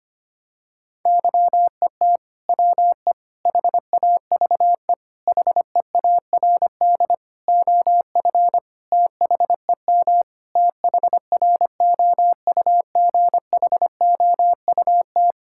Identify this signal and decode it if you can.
Morse code, heard as a keyed tone, about 25 wpm, 700 Hz. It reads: YET WE HAVE HEARD OF THEM THROUGHOUT
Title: Sentences from Top 1000 Words - Review Part 5 - 25wpm